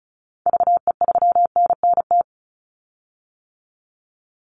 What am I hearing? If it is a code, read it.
Morse code, heard as a keyed tone, about 34 wpm, 700 Hz. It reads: VE3NNT
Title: Call Signs - 35wpm